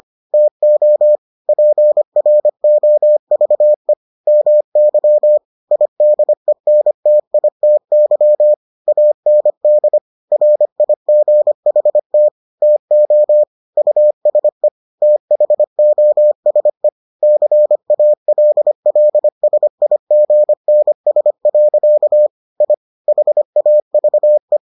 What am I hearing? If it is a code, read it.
Morse code, heard as a keyed tone, about 25 wpm, 600 Hz. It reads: TO PROVE MY IDENTITY AND RIGHT TO USE THOSE CALLSIGNS. I HAVE